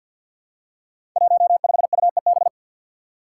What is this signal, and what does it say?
Morse code, heard as a keyed tone, about 51 wpm, 700 Hz. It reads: 15UEB
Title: Five Challenging Letter-Number Combinations 50wpm